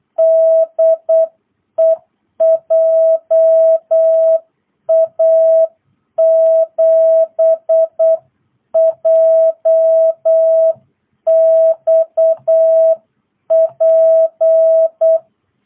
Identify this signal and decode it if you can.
Morse code, heard as a keyed tone, about 8 wpm, 660 Hz. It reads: DE JA7JXP